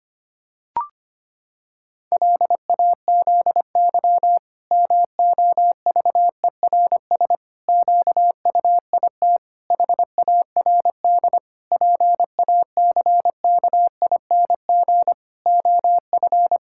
Morse code, heard as a keyed tone, about 25 wpm, 700 Hz. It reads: LAZY MOVERS QUIT HARD PACKING OF
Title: Pangram Challenge v2 - 25wpm